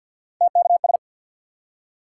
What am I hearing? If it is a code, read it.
Morse code, heard as a keyed tone, about 47 wpm, 700 Hz. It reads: TKS
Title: Top 90 QSO Elements 50wpm